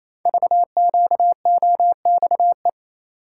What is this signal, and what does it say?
Morse code, heard as a keyed tone, about 29 wpm, 700 Hz. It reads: VQOXE
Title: Code Groups - 5 Letters - 28wpm